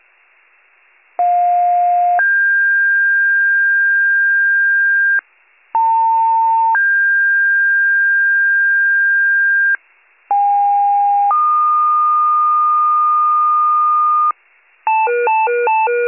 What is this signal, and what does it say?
Two-tone page, heard as a one-second tone, then a three-second tone, about 710 then 1600 Hz, about 900 then 1600 Hz, about 810 then 1220 Hz, with noise. It is heard from 1.2 to 14.3 s.